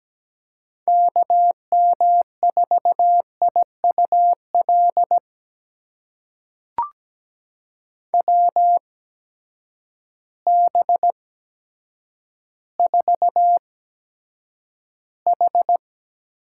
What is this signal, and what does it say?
Morse code, heard as a keyed tone, about 17 wpm, spaced as for 8 wpm, 700 Hz. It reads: KM4IUL WB4H
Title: USA Call Signs - 6 Characters - 8x Character Spacing - 17wpm